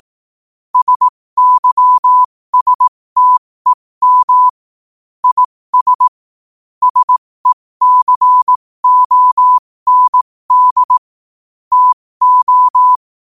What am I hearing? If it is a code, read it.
Morse code, heard as a keyed tone, about 18 wpm, spaced as for 15 wpm, 1000 Hz. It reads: SYSTEM IS SECOND TO
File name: qotdfiles / 15wpm